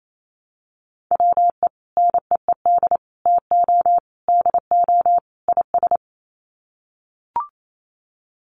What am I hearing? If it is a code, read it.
Morse code, heard as a keyed tone, about 28 wpm, 700 Hz. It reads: WE NEED TO DO IS
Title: Sets of 5 Words Encore2 28wpm